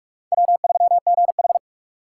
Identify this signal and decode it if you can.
Morse code, heard as a keyed tone, about 46 wpm, 700 Hz. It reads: W3GH
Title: USA Call Signs - 2x Character Spacing - 45wpm